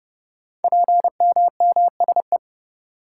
Morse code, heard as a keyed tone, about 28 wpm, 700 Hz. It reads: PMMSE
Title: Code Groups - 5 Letters - 30wpm